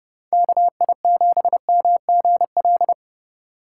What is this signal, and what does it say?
Morse code, heard as a keyed tone, about 36 wpm, 700 Hz. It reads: KI7MGL
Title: Speed Racing - USA 2x3 Call Signs 30wpm